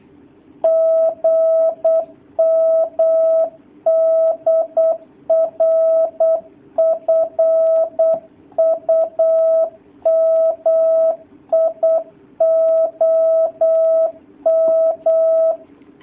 Morse code, heard as a keyed tone, about 8 wpm, 650 Hz. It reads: GM DR FUMI OM
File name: jh8cbh / 2019cw / sp08- w